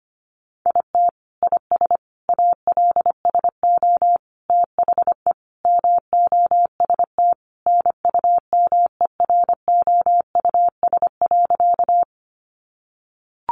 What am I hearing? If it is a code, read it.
Morse code, heard as a keyed tone, about 25 wpm, 700 Hz. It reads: IT IS ALSO THE MOST NUMEROUS.
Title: Sentences from Top 1400 Words 25wpm